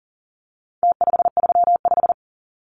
Morse code, heard as a keyed tone, about 40 wpm, 700 Hz. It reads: T535
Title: Numbers - Contest Style - 4-Digit 40wpm